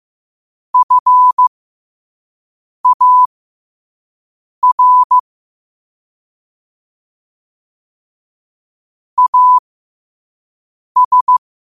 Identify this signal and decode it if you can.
Morse code, heard as a keyed tone, about 15 wpm, spaced as for 5 wpm, 1000 Hz. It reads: FAR AS
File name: qotdfiles / 5wpm